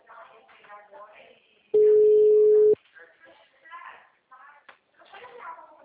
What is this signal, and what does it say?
Ringback tone (Europe, 425 Hz), with speech.
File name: out- -2041- - - .16509